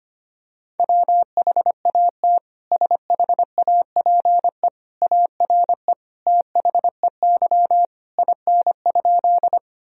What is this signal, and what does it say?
Morse code, heard as a keyed tone, about 25 wpm, 700 Hz. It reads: WHAT SHAPE ARE THEY IN?
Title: Sentences from Top 1000 Words - Review Part 3 - 25wpm